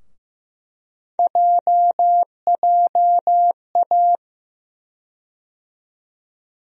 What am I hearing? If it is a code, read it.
Morse code, heard as a keyed tone, about 15 wpm, 700 Hz. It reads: JJA
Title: USA Call Sign Postfixes - 3 Letters 15wpm